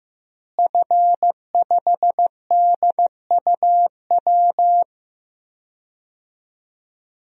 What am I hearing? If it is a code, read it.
Morse code, heard as a keyed tone, about 16 wpm, 700 Hz. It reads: F5DUW
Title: DXCC-100 Call Signs - 5 Characters - Repeated 2x - 15wpm